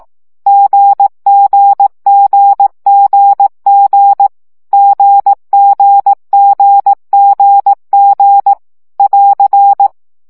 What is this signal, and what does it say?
Morse code, heard as a keyed tone, about 18 wpm, 800 Hz. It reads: GGGGG GGGGG <AR>
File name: CW gyakorlas g 15 baud